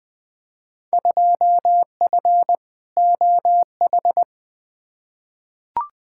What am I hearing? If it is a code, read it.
Morse code, heard as a keyed tone, about 21 wpm, 700 Hz. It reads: 2F OH
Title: Field Day Exchanges - 20wpm